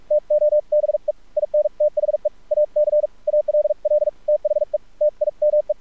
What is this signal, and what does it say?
Morse code, heard as a keyed tone, about 46 wpm, 600 Hz, with radio noise. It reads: TO BE IN THE AC ALL THE TIME
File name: archived / CRSnet